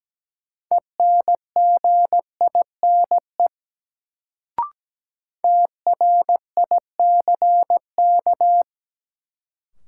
Morse code, heard as a keyed tone, about 17 wpm, 700 Hz. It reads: ENGINE TRICK
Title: Sentences from Top 1500 Words - New Word Drill Alt - 17wpm